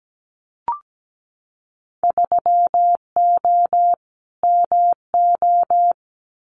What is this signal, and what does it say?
Morse code, heard as a keyed tone, about 17 wpm, 700 Hz. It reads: 3O MO
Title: Winter Field Day Exchanges Encore 17wpm